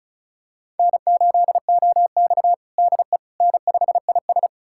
Morse code, heard as a keyed tone, about 36 wpm, 700 Hz. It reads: N8OX DE N5IS
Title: USA Callsign DE Callsign - 4 Characters - 35wpm